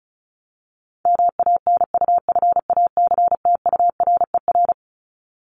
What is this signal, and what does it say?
Morse code, heard as a keyed tone, about 34 wpm, 700 Hz. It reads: MANUFACTURER
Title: Top 1751-2000 Words 35wpm